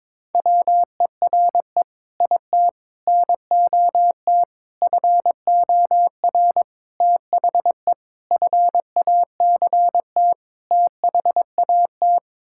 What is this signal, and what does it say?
Morse code, heard as a keyed tone, about 22 wpm, 700 Hz. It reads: WERE IT NOT FOR THE FACT THAT
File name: practice / 22wpm / Sets of 7 Words - 22wpm